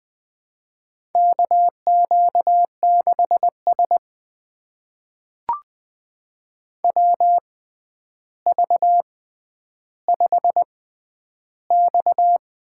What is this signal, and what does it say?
Morse code, heard as a keyed tone, about 20 wpm, spaced as for 10 wpm, 700 Hz. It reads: KQ6S WV5X
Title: USA Call Signs - 4 Characters - 6x Character Spacing - 20wpm